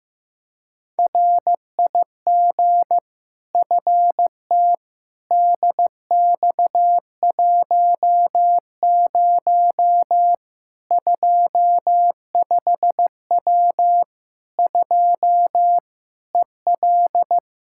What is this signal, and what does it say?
Morse code, heard as a keyed tone, about 15 wpm, 700 Hz. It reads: RIG FT DX10 25W 2 EL
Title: Practice QSO - Exchange 2 - Info - 15wpm